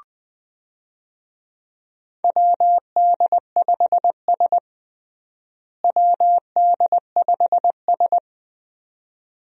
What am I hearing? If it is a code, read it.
Morse code, heard as a keyed tone, about 20 wpm, 700 Hz. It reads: WD5S WD5S
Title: USA Call Signs - 4 Characters - Repeated 2x - 20wpm